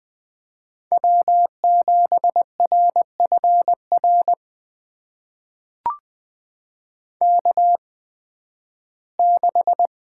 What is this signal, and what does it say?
Morse code, heard as a keyed tone, about 20 wpm, spaced as for 13 wpm, 700 Hz. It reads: W7RFR K6
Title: USA Call Signs - 5 Characters - 8x Character Spacing - 20wpm